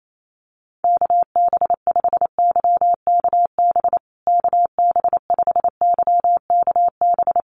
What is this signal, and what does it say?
Morse code, heard as a keyed tone, about 28 wpm, 700 Hz. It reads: KB5YKB KB5YKB
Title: USA Call Signs Repeated 2x 28wpm